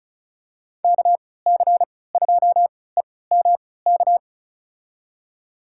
Morse code, heard as a keyed tone, about 36 wpm, spaced as for 20 wpm, 700 Hz. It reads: KC2EMK
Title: USA Call Signs - 6 Characters - 3x Character Spacing - 35wpm